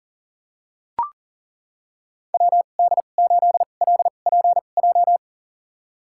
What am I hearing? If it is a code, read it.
Morse code, heard as a keyed tone, about 40 wpm, spaced as for 29 wpm, 700 Hz. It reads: WD8LPJ
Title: USA Call Signs - 6 Characters - 2x Character Spacing - 40wpm